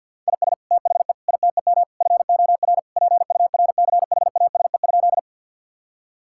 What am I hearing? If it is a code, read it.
Morse code, heard as a keyed tone, about 50 wpm, 700 Hz. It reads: IS THE ITEM FOR PURCHASE?
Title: Sentences from Top 1000 Words - Review Part 6 - 50wpm